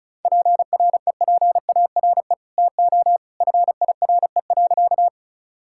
Morse code, heard as a keyed tone, about 34 wpm, 700 Hz. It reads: PREPARE TO FIRE.
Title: Sentences from Top 1000 Words - Review Part 6 - 35wpm